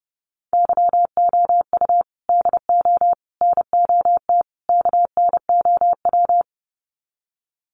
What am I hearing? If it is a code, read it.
Morse code, heard as a keyed tone, about 29 wpm, 700 Hz. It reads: YOU DO NOT KNOW
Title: Sets of 4 Words 30wpm